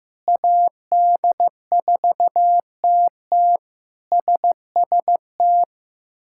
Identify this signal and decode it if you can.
Morse code, heard as a keyed tone, about 15 wpm, 700 Hz. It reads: AD4TT SST
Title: K1USN SST QSO Practice - 15wpm